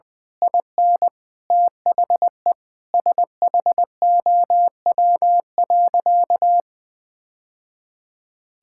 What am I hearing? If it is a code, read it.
Morse code, heard as a keyed tone, about 21 wpm, 700 Hz. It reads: IN THE SHOW.
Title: Sentences from Top 1000 Words - Review Part 6 - 20wpm